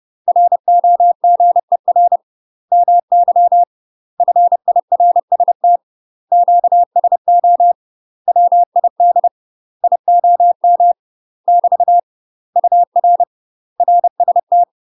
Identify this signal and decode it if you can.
Morse code, heard as a keyed tone, about 30 wpm, 700 Hz. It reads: ROGER MY FIRST QSO WID IOM <BT> UR RST
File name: UK MASTER QSOs TRAINING / 30WPM / UK MASTER QSOs 0026